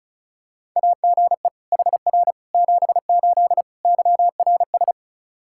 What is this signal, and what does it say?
Morse code, heard as a keyed tone, about 34 wpm, 700 Hz. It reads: AGE HR 78 YRS
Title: Practice QSO - Exchange 3 - More Info - 35wpm